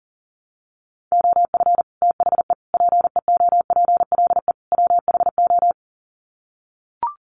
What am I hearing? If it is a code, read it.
Morse code, heard as a keyed tone, about 40 wpm, 700 Hz. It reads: OF THE PEOPLE WHO
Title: Sets of 4 Words 40wpm